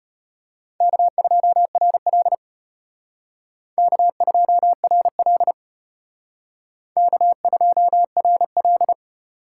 Morse code, heard as a keyed tone, about 32 wpm, 700 Hz. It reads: K2RL K2RL K2RL